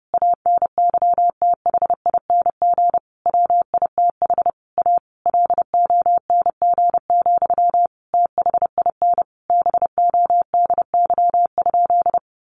Morse code, heard as a keyed tone, about 30 wpm, 700 Hz. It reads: ANYTHING WITH A LONG, THIN BODY?
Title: Sentences from Top 1200 Words 30wpm